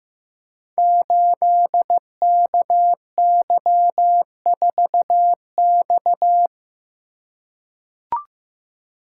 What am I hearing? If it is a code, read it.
Morse code, heard as a keyed tone, about 15 wpm, 700 Hz. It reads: 8KY4X
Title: Code Groups - 5 Letter:Numbers - 15wpm